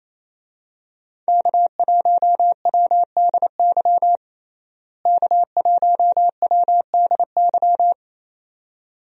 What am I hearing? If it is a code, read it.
Morse code, heard as a keyed tone, about 28 wpm, 700 Hz. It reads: K1WDY K1WDY
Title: USA Call Signs - 5 Characters - Repeated 2x - 28wpm